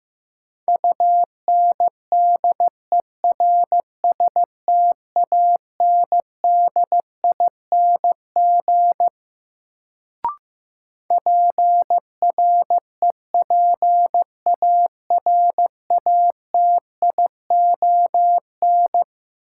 Morse code, heard as a keyed tone, about 15 wpm, 700 Hz. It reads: UNDERSTANDING PREPARATION
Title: Sentences from Top 1400 Words - New Word Drill Alt - 15wpm